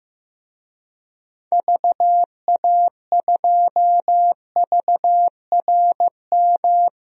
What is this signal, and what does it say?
Morse code, heard as a keyed tone, about 16 wpm, 700 Hz. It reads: VA2VRM
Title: DXCC-100 Call Signs - 6 Characters - 15wpm